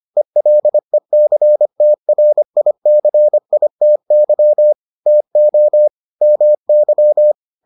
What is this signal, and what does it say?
Morse code, heard as a keyed tone, about 25 wpm, 600 Hz. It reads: ELECTRICITY TO MY